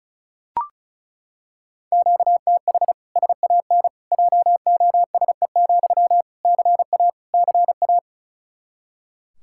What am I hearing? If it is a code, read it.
Morse code, heard as a keyed tone, about 36 wpm, 700 Hz. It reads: QTH SAN JOSE, CA CA
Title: Practice QSO - Exchange 1 - Beginning - 35wpm